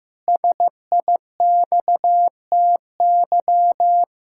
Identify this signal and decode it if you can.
Morse code, heard as a keyed tone, about 16 wpm, 700 Hz. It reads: SIXTY
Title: Pangram Challenge v2 - 15wpm